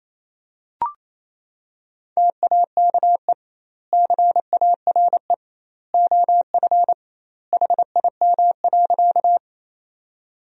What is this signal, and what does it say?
Morse code, heard as a keyed tone, about 28 wpm, 700 Hz. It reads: TAKE CARE OF HIM.
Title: 2x Word-Spacing - Sentences from Top 300 Words - 28wpm